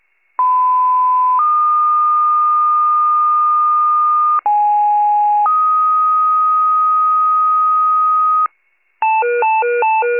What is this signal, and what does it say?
Two-tone page, heard as a one-second tone, then a three-second tone, about 1010 then 1220 Hz, about 830 then 1250 Hz. It is heard from 0.4 to 8.5 s.